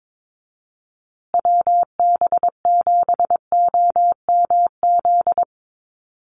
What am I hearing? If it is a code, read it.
Morse code, heard as a keyed tone, about 22 wpm, 700 Hz. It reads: WB7OMZ
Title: USA Call Signs Repeated 2x 22wpm